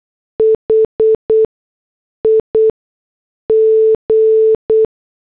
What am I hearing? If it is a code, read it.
Morse code, heard as a keyed tone, about 8 wpm, spaced as for 6 wpm, 440 Hz. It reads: HIG
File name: morse / headlines